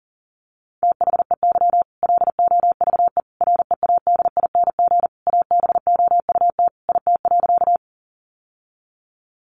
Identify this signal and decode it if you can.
Morse code, heard as a keyed tone, about 40 wpm, 700 Hz. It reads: THEY LOVE READING ABOUT IT.
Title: Sentences from Top 600 Words 40wpm